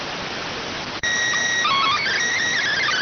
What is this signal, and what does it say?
SSTV (slow-scan television), PD 290.